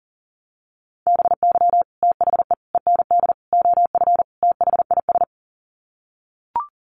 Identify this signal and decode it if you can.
Morse code, heard as a keyed tone, about 37 wpm, 700 Hz. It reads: BY THE END OF THIS
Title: Sets of 5 Words Encore 40wpm